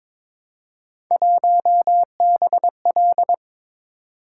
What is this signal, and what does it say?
Morse code, heard as a keyed tone, about 23 wpm, 700 Hz. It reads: 1BL
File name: practice / 22wpm / Three Letter-Number Combinations - 22wpm